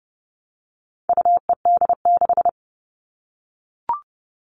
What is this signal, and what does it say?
Morse code, heard as a keyed tone, about 30 wpm, 700 Hz. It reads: UED6
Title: All Cut Numbers - 4-Digit 30wpm